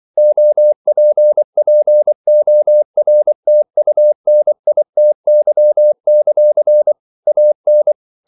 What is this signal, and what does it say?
Morse code, heard as a keyed tone, about 24 wpm, 600 Hz. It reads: OPPORTUNITY; AN
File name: CW / Practice / EnglishQuotations0424